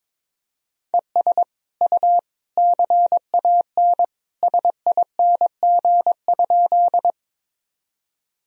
Morse code, heard as a keyed tone, about 22 wpm, 700 Hz. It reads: ES U CAN SING?
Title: Sentences from Top 400 Words and Abbreviations 22wpm